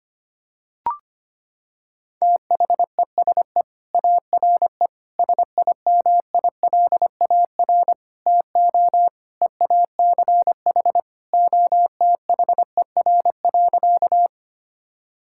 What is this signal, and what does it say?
Morse code, heard as a keyed tone, about 25 wpm, 700 Hz. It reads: THESE ARE SIMILAR TO EACH OTHER.
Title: Sentences from Top 1000 Words - Review Part 5 - 25wpm